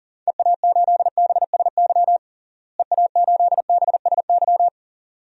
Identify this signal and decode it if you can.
Morse code, heard as a keyed tone, about 40 wpm, 700 Hz. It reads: EA8BSY EA8BSY
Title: DXCC-100 Call Signs - 6 Characters - Repeated 2x - 40wpm